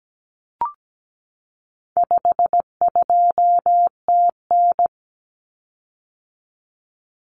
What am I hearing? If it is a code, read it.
Morse code, heard as a keyed tone, about 17 wpm, 700 Hz. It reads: 52TN
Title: Common Cut Numbers - 4-Digit 17wpm